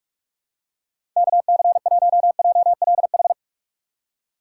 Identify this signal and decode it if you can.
Morse code, heard as a keyed tone, about 42 wpm, 700 Hz. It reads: KC1JLH
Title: USA Call Signs - 6 Characters - 6x Character Spacing - 45wpm